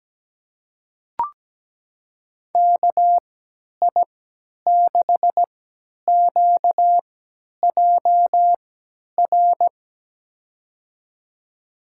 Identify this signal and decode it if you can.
Morse code, heard as a keyed tone, about 17 wpm, spaced as for 10 wpm, 700 Hz. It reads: KI6QJR
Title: USA Call Signs - 3x Character Spacing - 17wpm